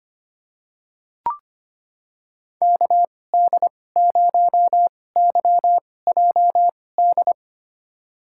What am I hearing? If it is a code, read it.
Morse code, heard as a keyed tone, about 25 wpm, spaced as for 18 wpm, 700 Hz. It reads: KD0YJD
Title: USA Call Signs - 6 Characters - 2x Character Spacing - 25wpm